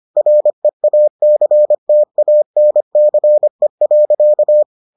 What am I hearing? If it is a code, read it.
Morse code, heard as a keyed tone, about 25 wpm, 600 Hz. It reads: REACTANCE.